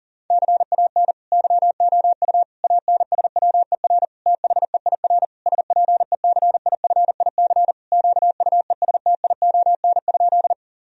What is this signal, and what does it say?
Morse code, heard as a keyed tone, about 40 wpm, 700 Hz. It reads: CAN YOU ANSWER THEIR SPECIFIC QUESTION?
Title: Sentences from Top 1000 Words - Review Part 3 - 40wpm